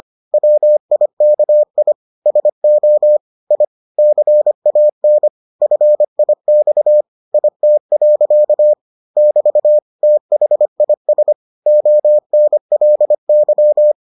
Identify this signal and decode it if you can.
Morse code, heard as a keyed tone, about 25 wpm, 600 Hz. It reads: WIKI SO I CAN FIX IT. <BT> THIS ONLY